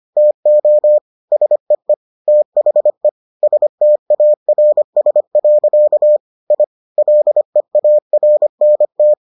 Morse code, heard as a keyed tone, about 25 wpm, 600 Hz. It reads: TO SEE THE STARS. I LEARNT